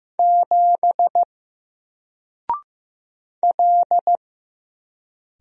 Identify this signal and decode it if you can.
Morse code, heard as a keyed tone, about 15 wpm, 700 Hz. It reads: 7 L
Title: Single Letter-Number 15wpm